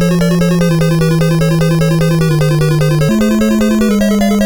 There is no code.